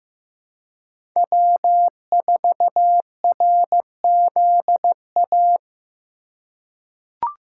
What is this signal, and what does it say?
Morse code, heard as a keyed tone, about 15 wpm, 700 Hz. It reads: W4RZA
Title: USA Call Signs - 2x Character Spacing - 15wpm